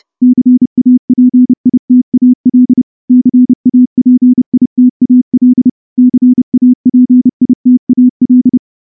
Morse code, heard as a keyed tone, about 28 wpm, 250 Hz. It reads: CAPITAL CAPITAL CAPITAL
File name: morse-training-30WPM